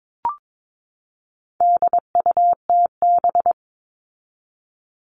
Morse code, heard as a keyed tone, about 22 wpm, 700 Hz. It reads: DUTB
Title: All Cut Numbers - 4-Digit 22wpm